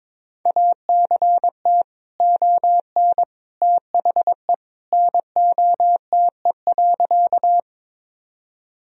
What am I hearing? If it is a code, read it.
Morse code, heard as a keyed tone, about 22 wpm, 700 Hz. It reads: ACT ON THE NOTE.
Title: Sentences from Top 1000 Words - Review Part 2 - 22wpm